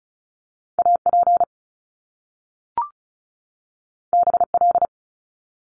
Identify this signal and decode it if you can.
Morse code, heard as a keyed tone, about 36 wpm, 700 Hz. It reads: AP BL
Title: Top Two Letter Combinations in Words 35wpm